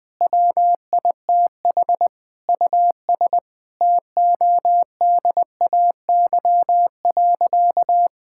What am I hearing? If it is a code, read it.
Morse code, heard as a keyed tone, about 20 wpm, 700 Hz. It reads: WITH US TODAY.
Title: Sentences with 7-Word Phrases - 20wpm